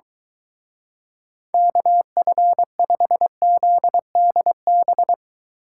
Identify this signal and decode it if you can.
Morse code, heard as a keyed tone, about 22 wpm, 700 Hz. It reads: KF5ZDB